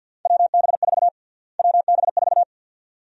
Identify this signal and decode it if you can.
Morse code, heard as a keyed tone, about 52 wpm, 700 Hz. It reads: WB4 WB4
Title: USA Call Sign Prefixes - Repeated 2x - 50wpm